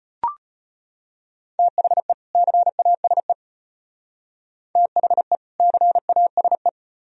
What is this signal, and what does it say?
Morse code, heard as a keyed tone, about 32 wpm, 700 Hz. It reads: THE CASE THE CASE